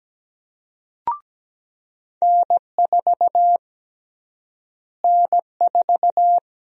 Morse code, heard as a keyed tone, about 17 wpm, 700 Hz. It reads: N4 N4
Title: USA Call Sign Prefixes - Repeated 2x - 17wpm